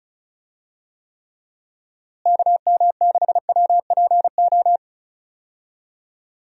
Morse code, heard as a keyed tone, about 36 wpm, 700 Hz. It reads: KM6WPO
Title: USA Call Signs - 6 Characters - Repeated 2x - 35wpm